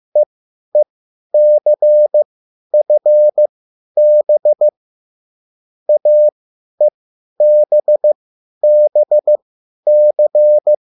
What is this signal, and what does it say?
Morse code, heard as a keyed tone, about 15 wpm, spaced as for 10 wpm, 600 Hz. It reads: EECFB AEBBC
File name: cwbook